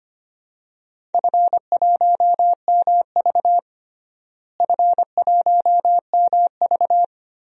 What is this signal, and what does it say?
Morse code, heard as a keyed tone, about 25 wpm, 700 Hz. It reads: F1MV F1MV
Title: DXCC-100 Call Signs - 4 Characters - Repeated 2x - 25wpm